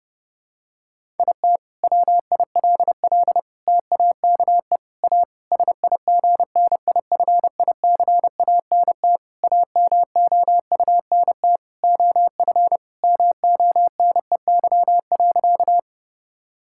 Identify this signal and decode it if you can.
Morse code, heard as a keyed tone, about 30 wpm, 700 Hz. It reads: IT WILL TAKE A SIGNIFICANT AMOUNT OF MONEY.
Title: Sentences from Top 1000 Words - Review Part 6 - 30wpm